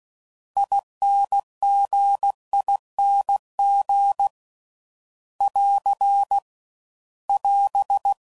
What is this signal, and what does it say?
Morse code, heard as a keyed tone, about 16 wpm, 800 Hz. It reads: INGING <AR> <AS>